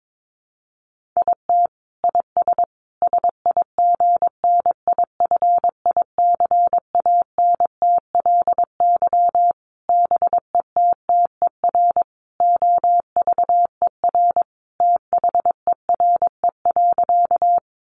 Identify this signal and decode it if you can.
Morse code, heard as a keyed tone, about 22 wpm, 700 Hz. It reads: IT IS SIGNIFICANTLY BETTER OVER THERE.
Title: Sentences from Top 1500 Words 22wpm